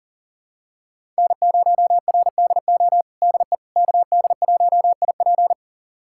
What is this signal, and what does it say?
Morse code, heard as a keyed tone, about 40 wpm, 700 Hz. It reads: N0RDO DE KD1IP
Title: USA Callsign DE Callsign - 5 Characters - 40wpm